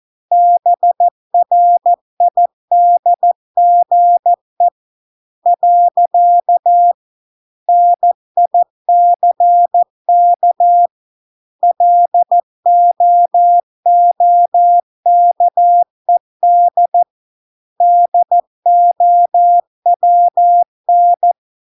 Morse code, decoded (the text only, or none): BRIDGE . NICK LOOKED DOWN